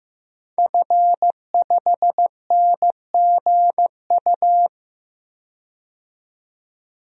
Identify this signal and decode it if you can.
Morse code, heard as a keyed tone, about 15 wpm, 700 Hz. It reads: F5NGU
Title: DXCC-100 Call Signs - 5 Characters - 15wpm